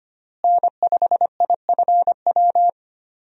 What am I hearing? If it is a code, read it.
Morse code, heard as a keyed tone, about 25 wpm, 700 Hz. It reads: N5IFW
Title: USA Call Signs - 5 Characters - Repeated 2x - 25wpm